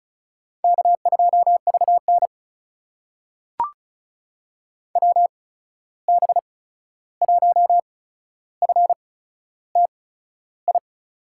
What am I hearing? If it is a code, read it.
Morse code, heard as a keyed tone, about 36 wpm, spaced as for 13 wpm, 700 Hz. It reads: K2VN WB1FTI
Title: USA Call Signs - 8x Character Spacing - 35wpm